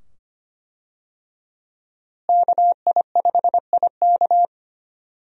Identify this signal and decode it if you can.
Morse code, heard as a keyed tone, about 25 wpm, 700 Hz. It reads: KI5IK
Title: USA Call Signs - 5 Characters - Repeated 2x - 25wpm